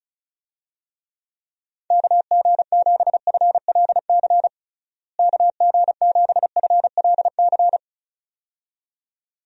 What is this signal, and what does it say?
Morse code, heard as a keyed tone, about 36 wpm, 700 Hz. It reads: KG7FLC KG7FLC
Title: USA Call Signs - 6 Characters - Repeated 2x - 35wpm